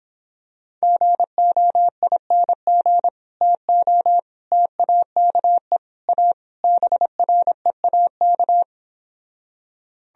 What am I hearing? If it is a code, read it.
Morse code, heard as a keyed tone, about 26 wpm, 700 Hz. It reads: GOING TO TAKE A BREAK